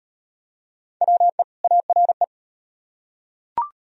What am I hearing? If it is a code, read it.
Morse code, heard as a keyed tone, about 38 wpm, 700 Hz. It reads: WE ARE